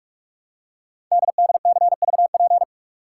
Morse code, heard as a keyed tone, about 44 wpm, 700 Hz. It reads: DDCVP
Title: Code Groups - 5 Letters - 45wpm